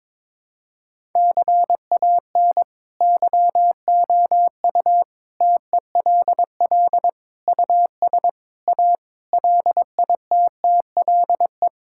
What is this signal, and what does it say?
Morse code, heard as a keyed tone, about 22 wpm, 700 Hz. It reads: CAN YOU TELL US A LITTLE
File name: practice / 22wpm / Sentences with 8-Word Phrases - 22wpm